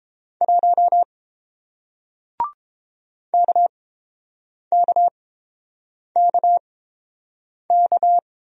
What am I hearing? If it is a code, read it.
Morse code, heard as a keyed tone, about 31 wpm, spaced as for 7 wpm, 700 Hz. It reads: 1 KKKK